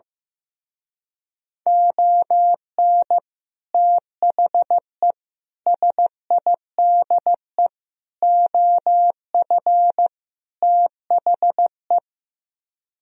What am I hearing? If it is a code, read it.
Morse code, heard as a keyed tone, about 15 wpm, 700 Hz. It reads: ON THE SIDE OF THE
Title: Speed-Racing - Sets of 5 Words 15wpm